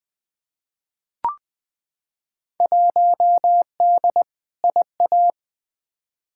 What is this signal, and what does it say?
Morse code, heard as a keyed tone, about 20 wpm, 700 Hz. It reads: 1D IA
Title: Field Day Exchanges - 20wpm